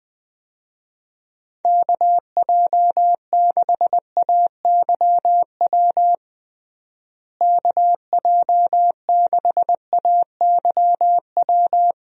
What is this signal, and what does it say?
Morse code, heard as a keyed tone, about 19 wpm, 700 Hz. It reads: KJ6AYW KJ6AYW
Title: USA Call Signs - 6 Characters - Repeated 3x - 20wpm